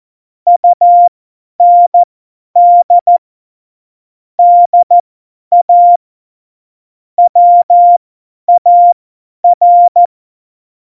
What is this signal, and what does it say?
Morse code, heard as a keyed tone, about 14 wpm, spaced as for 10 wpm, 700 Hz. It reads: UND DA WAR
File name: Morsekurs / Witze / Witz 14. November 2017, 50 F70